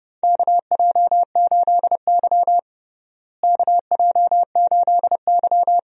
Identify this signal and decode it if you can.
Morse code, heard as a keyed tone, about 32 wpm, 700 Hz. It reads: KJ8Y KJ8Y
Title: USA Call Signs - 4 Characters - Repeated 2x - 30wpm